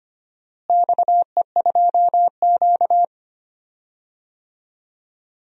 Morse code, heard as a keyed tone, about 25 wpm, 700 Hz. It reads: XE2Q
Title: DXCC-100 Call Signs - 4 Characters - Repeated 3x - 25wpm